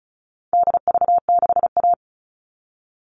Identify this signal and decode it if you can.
Morse code, heard as a keyed tone, about 34 wpm, 700 Hz. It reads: DV6A
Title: All Cut Numbers - 4-Digit 35wpm